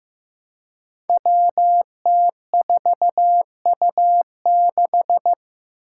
Morse code, heard as a keyed tone, about 15 wpm, 700 Hz. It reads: WT4U6
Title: Five Challenging Letter-Number Combinations 15wpm